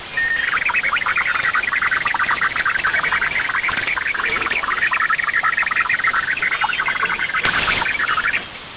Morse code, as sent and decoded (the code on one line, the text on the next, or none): none